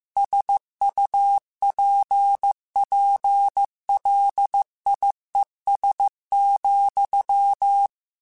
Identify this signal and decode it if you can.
Morse code, heard as a keyed tone, about 15 wpm, 800 Hz. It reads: SUPPLIES,
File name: Technical Subjects / Technical Subjects-mp3s / Sw Pwr Sup @13